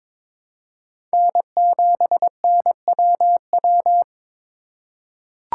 Morse code, heard as a keyed tone, about 22 wpm, 700 Hz. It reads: N7NWW
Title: USA Call Signs - 5 Characters - 4x Character Spacing - 22wpm